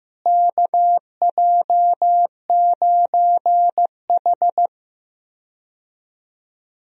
Morse code, heard as a keyed tone, about 16 wpm, 700 Hz. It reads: KJ9H
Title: USA Call Signs - 4 Characters - Repeated 2x - 15wpm